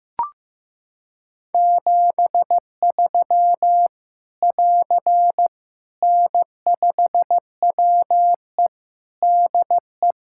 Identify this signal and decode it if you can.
Morse code, heard as a keyed tone, about 15 wpm, 700 Hz. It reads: 73 <AR> N5WE DE
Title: K1USN SST QSO Practice 15wpm